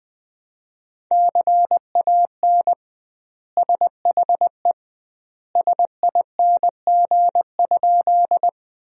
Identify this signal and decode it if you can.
Morse code, heard as a keyed tone, about 20 wpm, 700 Hz. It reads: CAN SHE SING?
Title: 2x Word-Spacing - Sentences from Top 400 Words - 20wpm